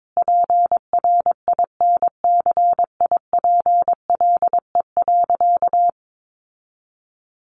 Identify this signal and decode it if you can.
Morse code, heard as a keyed tone, about 22 wpm, 700 Hz. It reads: PRINCIPLE.
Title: Sentences from Top 1400 Words 22wpm